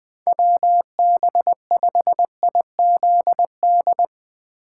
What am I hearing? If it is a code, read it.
Morse code, heard as a keyed tone, about 20 wpm, 700 Hz. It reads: WB5IZD
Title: USA Call Signs - 6 Characters - 2x Character Spacing - 20wpm